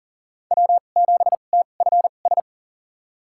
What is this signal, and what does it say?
Morse code, heard as a keyed tone, about 40 wpm, spaced as for 29 wpm, 700 Hz. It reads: W7TFS
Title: USA Call Signs - 2x Character Spacing - 40wpm